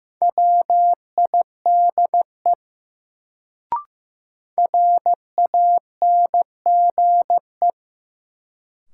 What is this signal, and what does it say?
Morse code, heard as a keyed tone, about 15 wpm, 700 Hz. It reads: WIDE RANGE
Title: Sets of 4 Words - New Word Drill - 15wpm